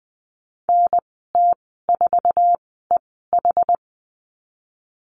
Morse code, heard as a keyed tone, about 20 wpm, spaced as for 15 wpm, 700 Hz. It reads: NT4EH
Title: 062 Lesson 11g - Letter h - Call Signs 20wpm